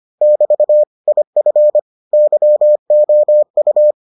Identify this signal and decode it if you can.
Morse code, heard as a keyed tone, about 25 wpm, 600 Hz. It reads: <BT> IF YOU